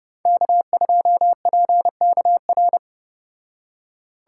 Morse code, heard as a keyed tone, about 30 wpm, 700 Hz. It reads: K2PKR